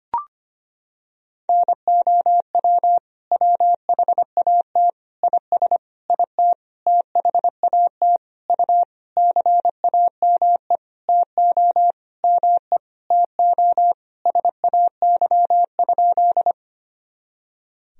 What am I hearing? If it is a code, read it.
Morse code, heard as a keyed tone, about 25 wpm, 700 Hz. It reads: NOW WHAT IS IT THAT U CAME TO ME TO SAY?
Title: Sentences from Top 200 Words and Abbreviations 25wpm